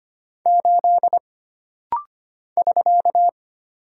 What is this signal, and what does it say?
Morse code, heard as a keyed tone, about 25 wpm, 700 Hz. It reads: 8 <SK>
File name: Single Character (Letter:Number:Punctuation:Prosign) - Rapid-Fire - 25wpm